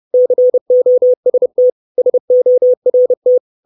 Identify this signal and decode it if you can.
Morse code, heard as a keyed tone, about 30 wpm, 500 Hz. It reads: COST SORT
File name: 632 Words 30WPM